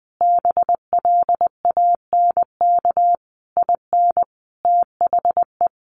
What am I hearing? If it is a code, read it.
Morse code, heard as a keyed tone, about 20 wpm, 700 Hz. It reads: BLANK IN THE
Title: Sentences from Top 1500 Words 20wpm